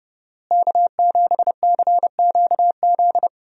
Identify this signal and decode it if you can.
Morse code, heard as a keyed tone, about 30 wpm, 700 Hz. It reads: K7CQZ
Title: USA Call Signs - 5 Characters - Repeated 3x - 30wpm